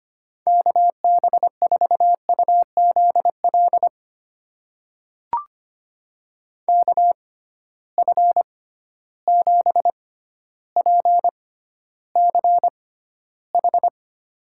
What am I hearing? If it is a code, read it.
Morse code, heard as a keyed tone, about 25 wpm, spaced as for 13 wpm, 700 Hz. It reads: KB4UZL KF7PCH
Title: USA Call Signs - 6 Characters - 6x Character Spacing - 25wpm